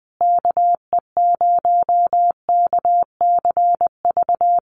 Morse code, heard as a keyed tone, about 20 wpm, 700 Hz. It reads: KE0KCV
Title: USA Call Signs Repeated 3x 20wpm